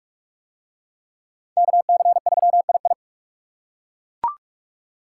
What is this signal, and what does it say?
Morse code, heard as a keyed tone, about 39 wpm, 700 Hz. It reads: KC3II